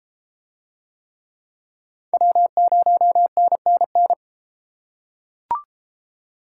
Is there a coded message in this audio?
Yes: Morse code from 2.1 to 4.1 s, heard as a keyed tone, about 28 wpm, 700 Hz. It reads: W0NNN